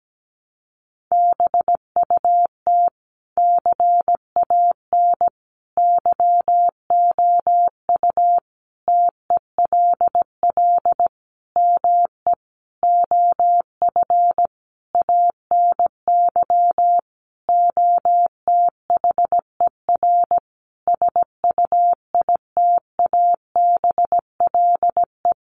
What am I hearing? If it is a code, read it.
Morse code, heard as a keyed tone, about 17 wpm, 700 Hz. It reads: BUT CAN YOU TELL ME OF ANY OTHER SUITABLE
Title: Sentences from Top 1400 Words 17wpm